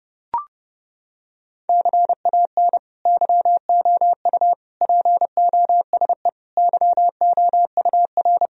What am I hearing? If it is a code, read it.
Morse code, heard as a keyed tone, about 30 wpm, 700 Hz. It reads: CAN YOU POSE YOUR
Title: Sentences from Top 1000 Words - Review Part 2 - 30wpm